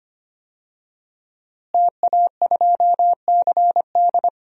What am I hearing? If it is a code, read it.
Morse code, heard as a keyed tone, about 25 wpm, 700 Hz. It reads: TA2CD
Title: DXCC-100 Call Signs - 5 Characters - Repeated 2x - 25wpm